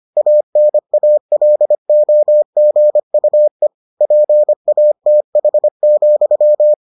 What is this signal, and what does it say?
Morse code, heard as a keyed tone, about 25 wpm, 600 Hz. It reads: ANALOGUE PATH,